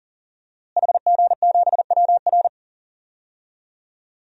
Morse code, heard as a keyed tone, about 39 wpm, 700 Hz. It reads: HG7WR
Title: DXCC-100 Call Signs - 5 Characters - Repeated 2x - 40wpm